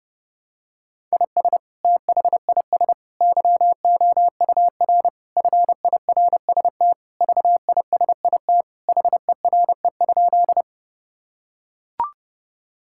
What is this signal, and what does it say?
Morse code, heard as a keyed tone, about 30 wpm, 700 Hz. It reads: IS THIS YOUR FIRST VISIT HERE?
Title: Sentences from Top 1000 Words - Review Part 6 - 30wpm